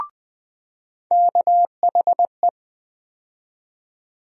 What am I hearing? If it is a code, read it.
Morse code, heard as a keyed tone, about 20 wpm, 700 Hz. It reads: KHE
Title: USA Call Sign Postfixes - 3 Letters 20wpm